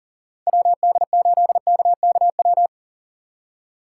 Morse code, heard as a keyed tone, about 49 wpm, 700 Hz. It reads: WD8KKW
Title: Speed Racing - USA 2x3 Call Signs 40wpm